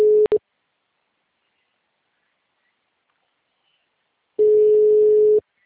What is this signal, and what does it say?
Ringback tone (Europe, 425 Hz).